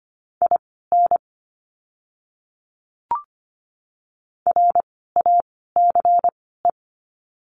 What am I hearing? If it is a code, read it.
Morse code, heard as a keyed tone, about 25 wpm, spaced as for 16 wpm, 700 Hz. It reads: IN RACE
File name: course / 25:10wpm / 094 Lesson 16f - Letter c - Words 25wpm